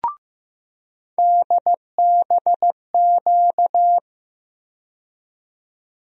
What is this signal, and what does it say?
Morse code, heard as a keyed tone, about 15 wpm, 700 Hz. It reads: DBQ
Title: USA Call Sign Postfixes - 3 Letters 15wpm